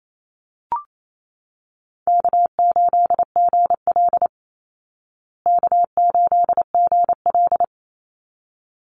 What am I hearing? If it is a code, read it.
Morse code, heard as a keyed tone, about 28 wpm, 700 Hz. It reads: K8GL K8GL
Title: Easy Call Signs Repeated 2x - 28wpm